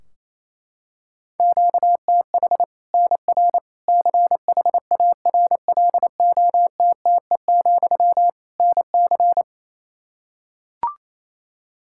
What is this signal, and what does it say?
Morse code, heard as a keyed tone, about 28 wpm, 700 Hz. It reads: QTH NR CHARLOTTE, NC
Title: Practice QSO - Exchange 1 - Beginning - 28wpm